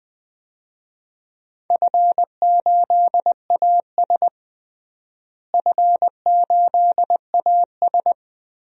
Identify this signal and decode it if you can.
Morse code, heard as a keyed tone, about 20 wpm, 700 Hz. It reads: F8AS F8AS
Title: DXCC-100 Call Signs - 4 Characters - Repeated 3x - 20wpm